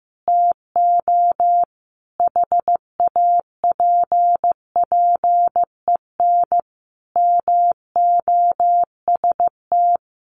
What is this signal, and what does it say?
Morse code, heard as a keyed tone, about 15 wpm, 700 Hz. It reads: TO HAPPEN MOST
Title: Sentences from Top 1300 Words 15wpm